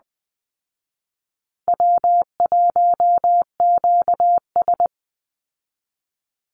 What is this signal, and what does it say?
Morse code, heard as a keyed tone, about 20 wpm, 700 Hz. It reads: W1QS
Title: Easy Call Signs Repeated 2x - 20wpm